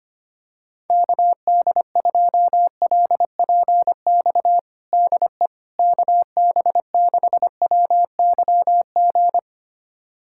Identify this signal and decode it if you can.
Morse code, heard as a keyed tone, about 25 wpm, 700 Hz. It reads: KD2LPX DE KB6WYG
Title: USA Callsign DE Callsign - 6 Characters - 25wpm